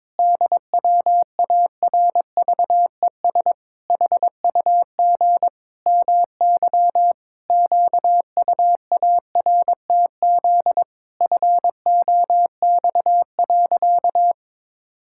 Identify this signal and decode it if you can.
Morse code, heard as a keyed tone, about 22 wpm, 700 Hz. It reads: DWARVES HUG MY QUARTZ FOX.
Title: Pangram Challenge v2 - 22wpm